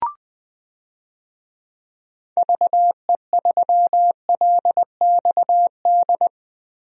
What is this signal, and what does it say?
Morse code, heard as a keyed tone, about 20 wpm, 700 Hz. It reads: VE3LXD
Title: DXCC-100 Call Signs - 6 Characters - Repeated 3x - 20wpm